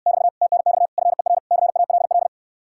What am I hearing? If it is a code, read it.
Morse code, heard as a keyed tone, about 69 wpm, 700 Hz. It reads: <BT> TNX FER CALL